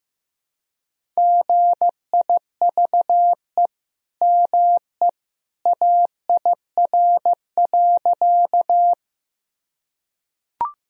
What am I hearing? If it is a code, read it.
Morse code, heard as a keyed tone, about 15 wpm, 700 Hz. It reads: GIVE ME AIR.
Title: Sentences from Top 1000 Words - Review Part 1 - 15wpm